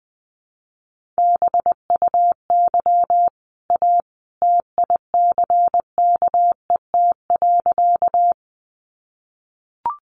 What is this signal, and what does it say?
Morse code, heard as a keyed tone, about 19 wpm, 700 Hz. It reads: BUY A TICKET.
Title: Sentences from Top 1400 Words 20wpm